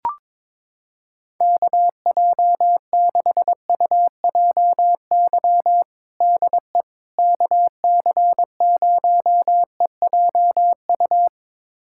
Morse code, heard as a keyed tone, about 22 wpm, 700 Hz. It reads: KJ6UJY DE KC0EJU
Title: USA Callsign DE Callsign - 6 Characters - 22wpm